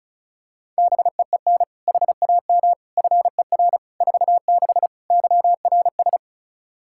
Morse code, heard as a keyed tone, about 34 wpm, 700 Hz. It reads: BEEN HAM FER 46 YRS
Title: Practice QSO - Exchange 3 - More Info - 35wpm